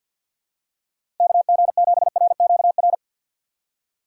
Morse code, heard as a keyed tone, about 52 wpm, 700 Hz. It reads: KG7RQR
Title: USA Call Signs - 6 Characters - 4x Character Spacing - 50wpm